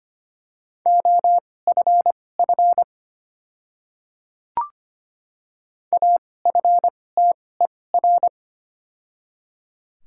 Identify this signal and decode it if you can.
Morse code, heard as a keyed tone, about 25 wpm, spaced as for 18 wpm, 700 Hz. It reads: OFF AFTER
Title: 2x Character Spacing - Top 500 Words - 25wpm